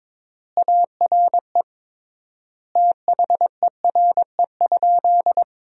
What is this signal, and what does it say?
Morse code, heard as a keyed tone, about 21 wpm, 700 Hz. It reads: ARE THERE?
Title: 3x Word-Spacing - Sentences from Top 100 Words - 22wpm